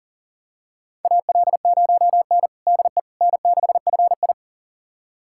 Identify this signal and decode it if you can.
Morse code, heard as a keyed tone, about 40 wpm, 700 Hz. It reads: AL0N DE N6FI
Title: USA Callsign DE Callsign - 4 Characters - 40wpm